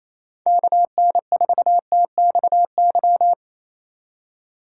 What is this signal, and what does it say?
Morse code, heard as a keyed tone, about 34 wpm, 700 Hz. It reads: KN4TXY
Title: Speed Racing - USA 2x3 Call Signs 28wpm